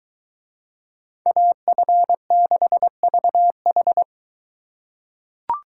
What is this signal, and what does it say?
Morse code, heard as a keyed tone, about 21 wpm, 700 Hz. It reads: AF6VH